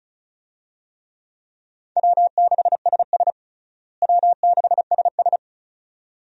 Morse code, heard as a keyed tone, about 36 wpm, 700 Hz. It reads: W6SS W6SS
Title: USA Call Signs - 4 Characters - Repeated 2x - 35wpm